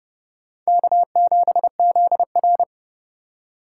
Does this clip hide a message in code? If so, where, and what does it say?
Yes: Morse code from 0.7 to 2.6 s, heard as a keyed tone, about 29 wpm, 700 Hz. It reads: K7ZR